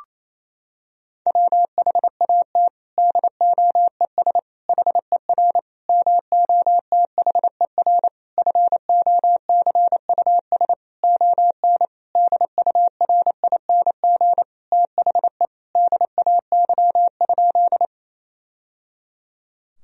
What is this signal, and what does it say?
Morse code, heard as a keyed tone, about 28 wpm, 700 Hz. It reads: WHAT DOES HER MOTHER FOCUS ON DURING THE DAY?
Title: Sentences from Top 1000 Words - Review Part 4 - 28wpm